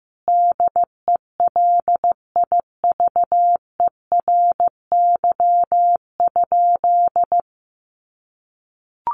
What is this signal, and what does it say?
Morse code, heard as a keyed tone, about 15 wpm, 700 Hz. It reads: DELIVERY?
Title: Sentences from Top 1500 Words 15wpm